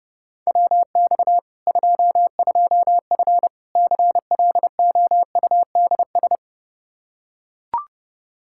Morse code, heard as a keyed tone, about 32 wpm, 700 Hz. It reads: WX 22F CLOUDS
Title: Practice QSO - Exchange 2 - Info - 30wpm